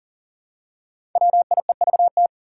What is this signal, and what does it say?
Morse code, heard as a keyed tone, about 38 wpm, 700 Hz. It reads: WIEVT
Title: Five Challenging Letter-Number Combinations 40wpm